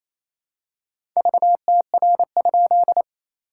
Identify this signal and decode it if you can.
Morse code, heard as a keyed tone, about 28 wpm, 700 Hz. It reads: VTR?
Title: SOTA QSO Practice 28wpm